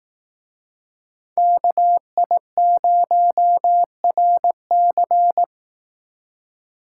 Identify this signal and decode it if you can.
Morse code, heard as a keyed tone, about 18 wpm, 700 Hz. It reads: KI0RC
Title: Speed Racing - USA 2x2 Call Signs 15wpm